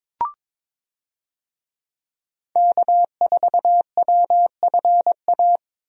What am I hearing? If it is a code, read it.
Morse code, heard as a keyed tone, about 22 wpm, 700 Hz. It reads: K4WFA
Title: USA Call Signs - 5 Characters - Repeated 2x - 22wpm